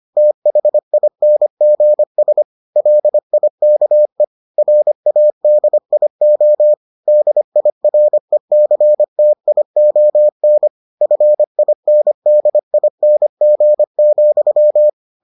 Morse code, heard as a keyed tone, about 25 wpm, 600 Hz. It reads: THINGS LIKE RADIO DIRECTION FINDING,